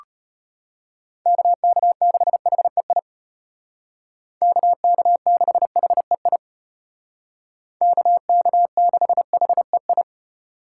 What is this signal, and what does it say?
Morse code, heard as a keyed tone, about 32 wpm, 700 Hz. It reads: KK6HEI KK6HEI KK6HEI